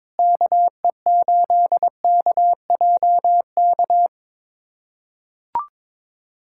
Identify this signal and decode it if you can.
Morse code, heard as a keyed tone, about 22 wpm, 700 Hz. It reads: KE8KJK
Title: USA Callsign DE Callsign 22wpm